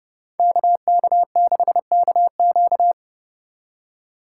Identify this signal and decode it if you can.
Morse code, heard as a keyed tone, about 30 wpm, 700 Hz. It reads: KK6KQ